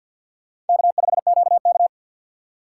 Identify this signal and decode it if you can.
Morse code, heard as a keyed tone, about 48 wpm, 700 Hz. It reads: K5QK
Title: USA Call Signs - 4 Characters - 3x Character Spacing - 50wpm